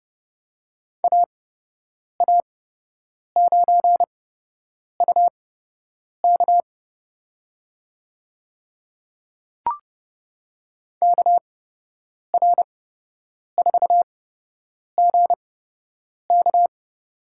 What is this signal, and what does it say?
Morse code, heard as a keyed tone, about 30 wpm, spaced as for 8 wpm, 700 Hz. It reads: AA9UK KR4GK
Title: USA Call Signs - 5 Characters - 8x Character Spacing - 30wpm